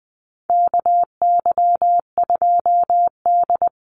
Morse code, heard as a keyed tone, about 20 wpm, 700 Hz. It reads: KY2D
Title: Call Signs - 20wpm